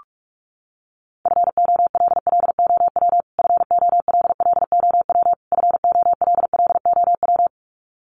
Morse code, heard as a keyed tone, about 46 wpm, 700 Hz. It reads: FOLLOW FOLLOW FOLLOW
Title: Top 300 Words Repeated 45wpm